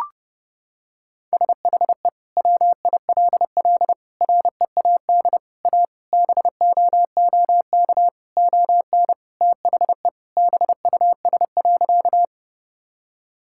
Morse code, heard as a keyed tone, about 32 wpm, 700 Hz. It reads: SHE WILL READ A BOOK ON THE BUS.
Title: Sentences from Top 1000 Words - Review Part 6 - 30wpm